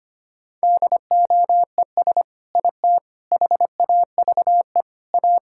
Morse code, heard as a keyed tone, about 25 wpm, 700 Hz. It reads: DOES IT HAVE A
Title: Sentences from Top 1000 Words - Review Part 5 - 25wpm